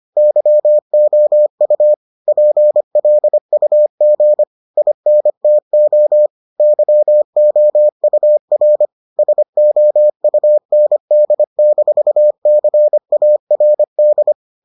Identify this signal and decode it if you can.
Morse code, heard as a keyed tone, about 25 wpm, 600 Hz. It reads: YOU PLUG INTO YOUR SOUND-CARD